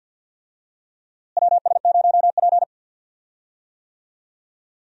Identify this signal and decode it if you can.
Morse code, heard as a keyed tone, about 46 wpm, 700 Hz. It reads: WS0P
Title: USA Call Signs - 4 Characters - Repeated 3x - 50wpm